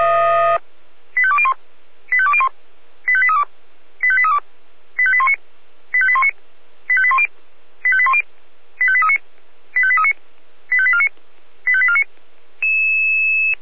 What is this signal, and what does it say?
Five-tone selective call (ZVEI), heard as quick runs of five tones, tones of about 70 ms. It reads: 85201 85201 85832 85832 85819 85819 85810 85810 85839 85839 85849 85849